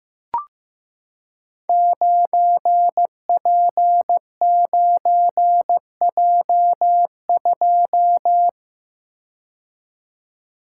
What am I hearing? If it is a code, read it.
Morse code, heard as a keyed tone, about 15 wpm, 700 Hz. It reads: 9P9J2
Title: Code Groups - 5 Letter:Numbers - 15wpm